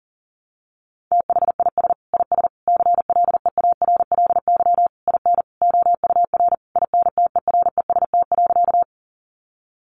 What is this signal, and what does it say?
Morse code, heard as a keyed tone, about 40 wpm, 700 Hz. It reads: THIS IS CLEARLY IN OUR INTEREST.
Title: Sentences from Top 1300 Words 40wpm